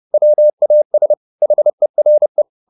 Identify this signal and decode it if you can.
Morse code, heard as a keyed tone, about 30 wpm, 600 Hz. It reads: WAS HERE